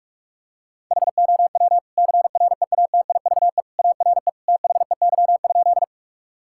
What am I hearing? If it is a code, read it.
Morse code, heard as a keyed tone, about 41 wpm, 700 Hz. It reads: HOW CREATIVE ARE THEY?
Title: Sentences from Top 1000 Words - Review Part 5 - 45wpm